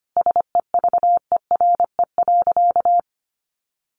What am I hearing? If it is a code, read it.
Morse code, heard as a keyed tone, about 25 wpm, 700 Hz. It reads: SEVERE.
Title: Sentences from Top 1400 Words 25wpm